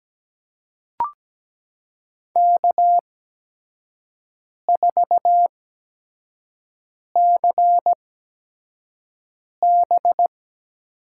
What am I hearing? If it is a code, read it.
Morse code, heard as a keyed tone, about 17 wpm, spaced as for 5 wpm, 700 Hz. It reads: K4CB
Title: USA Call Signs - 8x Character Spacing - 17wpm